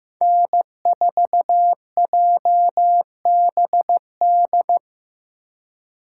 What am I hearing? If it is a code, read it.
Morse code, heard as a keyed tone, about 15 wpm, 700 Hz. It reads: N4JBD
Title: USA Call Signs - 5 Characters - 3x Character Spacing - 15wpm